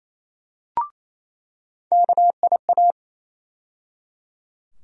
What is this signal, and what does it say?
Morse code, heard as a keyed tone, about 28 wpm, 700 Hz. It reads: KIA
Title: USA Call Sign Postfixes - 3 Letters 28wpm